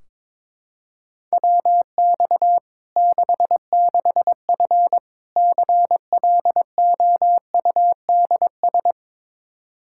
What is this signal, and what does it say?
Morse code, heard as a keyed tone, about 22 wpm, 700 Hz. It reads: WX 66F CLOUDS
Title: Practice QSO - Exchange 2 - Info - 22wpm